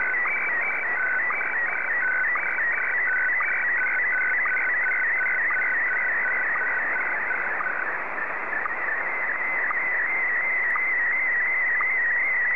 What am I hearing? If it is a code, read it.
SSTV (slow-scan television), Scottie DX.